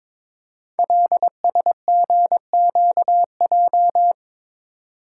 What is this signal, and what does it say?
Morse code, heard as a keyed tone, about 22 wpm, 690 Hz. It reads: LSGQJ
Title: Code Groups - 5 Letter:Numbers - 22wpm